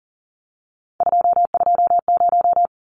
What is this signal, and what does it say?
Morse code, heard as a keyed tone, about 40 wpm, 700 Hz. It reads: 220
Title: Numbers - 3-Digit 40wpm